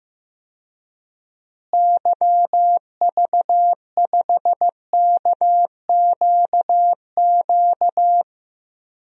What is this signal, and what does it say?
Morse code, heard as a keyed tone, about 15 wpm, 700 Hz. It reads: YV5KQQ
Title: DXCC-100 Call Signs - 6 Characters - Repeated 3x - 15wpm